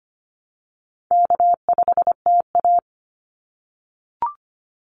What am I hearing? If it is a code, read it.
Morse code, heard as a keyed tone, about 25 wpm, 700 Hz. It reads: K5TA
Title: Easy Call Signs - 25wpm